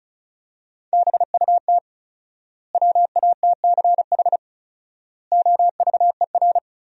Morse code, heard as a keyed tone, about 36 wpm, 700 Hz. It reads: BUT WATCH OVER
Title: 4x Word-Spacing - Sentences from Top 300 Words - 35wpm